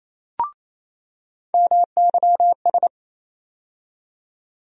Morse code, heard as a keyed tone, about 27 wpm, 700 Hz. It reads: MYS
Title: USA Call Sign Postfixes - 3 Letters 28wpm